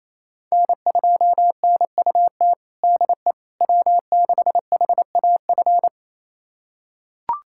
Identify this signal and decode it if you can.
Morse code, heard as a keyed tone, about 28 wpm, 700 Hz. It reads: N2NUT DE W6HAF
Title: USA Callsign DE Callsign - 5 Characters - 28wpm